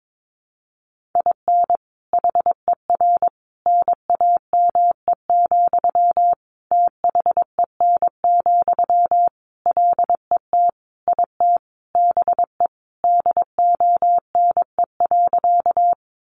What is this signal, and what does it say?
Morse code, heard as a keyed tone, about 22 wpm, 700 Hz. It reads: IN HER NAME, THEN, LET IT BE DONE.
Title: Sentences from Top 500 Words 22wpm